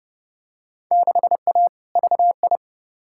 Morse code, heard as a keyed tone, about 30 wpm, 700 Hz. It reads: 6A VI
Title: Field Day Exchanges Encore - 30wpm